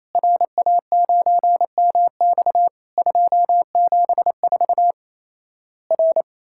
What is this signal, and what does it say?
Morse code, heard as a keyed tone, about 28 wpm, 700 Hz. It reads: RA9MX 274 R
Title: Worked All Europe DX Contest - QTC Exchange - 28wpm